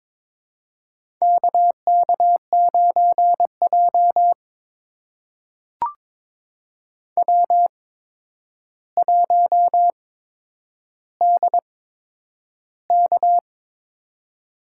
Morse code, heard as a keyed tone, about 22 wpm, spaced as for 9 wpm, 700 Hz. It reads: KK9J W1DK
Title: USA Call Signs - 4 Characters - 8x Character Spacing - 22wpm